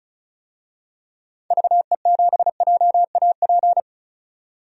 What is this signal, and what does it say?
Morse code, heard as a keyed tone, about 34 wpm, 700 Hz. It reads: VE7JAP
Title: DXCC-100 Call Signs - 6 Characters - 35wpm